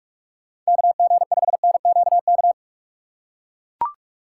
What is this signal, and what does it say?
Morse code, heard as a keyed tone, about 44 wpm, 700 Hz. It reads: KG5NQK
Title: USA Call Signs - 6 Characters - 6x Character Spacing - 45wpm